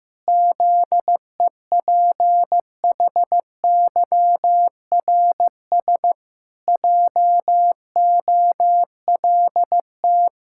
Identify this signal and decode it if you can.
Morse code, heard as a keyed tone, about 16 wpm, 700 Hz. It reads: ZEPHYRS JOLT
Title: Pangram Challenge v2 - 15wpm